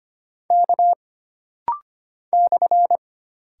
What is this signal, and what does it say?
Morse code, heard as a keyed tone, about 25 wpm, 700 Hz. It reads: K /
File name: Single Character (Letter:Number:Punctuation:Prosign) - Rapid-Fire - 25wpm